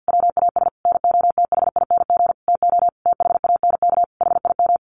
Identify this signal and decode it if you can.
Morse code, heard as a keyed tone, about 48 wpm, 700 Hz. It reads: WAS NOTHING TO THANK HIM
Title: Sentences from Top 1000 Words 50wpm